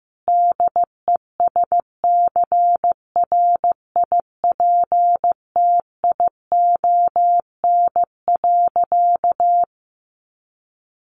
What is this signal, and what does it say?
Morse code, heard as a keyed tone, about 15 wpm, 700 Hz. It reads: DESCRIPTION.
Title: Sentences from Top 1300 Words 15wpm